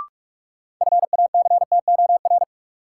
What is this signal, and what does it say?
Morse code, heard as a keyed tone, about 41 wpm, 700 Hz. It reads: FACTOR
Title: Sentences from Top 1300 Words - New Word Drill Alt - 45wpm